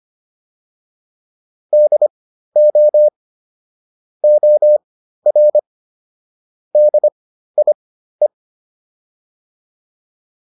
Morse code, heard as a keyed tone, about 25 wpm, spaced as for 13 wpm, 600 Hz. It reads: DO OR DIE